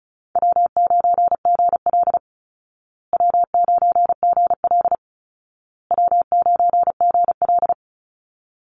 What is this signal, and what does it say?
Morse code, heard as a keyed tone, about 34 wpm, 700 Hz. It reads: W9GL W9GL W9GL
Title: Easy Call Signs Repeated 3x - 35wpm